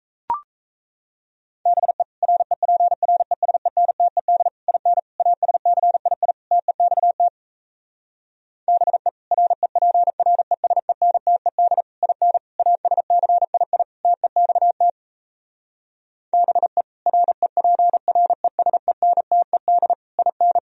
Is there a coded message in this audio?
Yes: Morse code from 1.7 to 20.6 s, heard as a keyed tone, about 37 wpm, 700 Hz. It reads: BE REPRESENTED IN ASCII TEXT BE REPRESENTED IN ASCII TEXT BE REPRESENTED IN